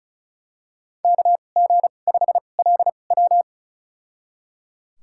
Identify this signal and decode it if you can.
Morse code, heard as a keyed tone, about 36 wpm, spaced as for 25 wpm, 700 Hz. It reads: KG5LW
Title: USA Call Signs - 5 Characters - 2x Character Spacing - 35wpm